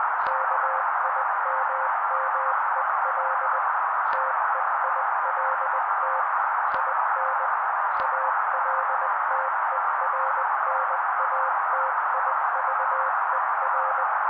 Morse code, heard as a keyed tone, about 20 wpm, 530 Hz. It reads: KIMMEL TEILT IN ALTERNATIVER
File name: daily / 18wpm noise